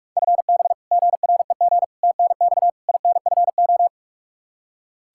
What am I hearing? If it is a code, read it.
Morse code, heard as a keyed tone, about 46 wpm, 700 Hz. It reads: FB GREG TNX INFO
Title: Practice QSO - Exchange 3 - More Info - 45wpm